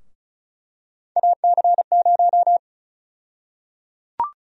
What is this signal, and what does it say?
Morse code, heard as a keyed tone, about 36 wpm, 700 Hz. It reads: AC0
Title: USA Call Sign Prefixes - Repeated 2x - 35wpm